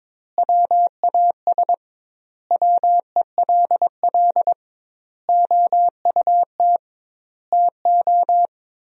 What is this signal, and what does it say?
Morse code, heard as a keyed tone, about 22 wpm, 700 Hz. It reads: WAS WELL OUT TO
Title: 2x Word-Spacing - Sentences from Top 400 Words - 22wpm